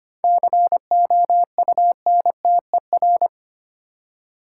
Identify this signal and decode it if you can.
Morse code, heard as a keyed tone, about 25 wpm, 700 Hz. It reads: COUNTER
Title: Seven Letter Words - 25wpm